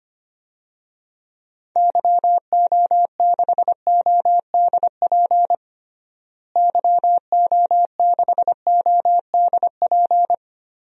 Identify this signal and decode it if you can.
Morse code, heard as a keyed tone, about 25 wpm, 700 Hz. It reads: YO6ODP YO6ODP
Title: DXCC-100 Call Signs - 6 Characters - Repeated 2x - 25wpm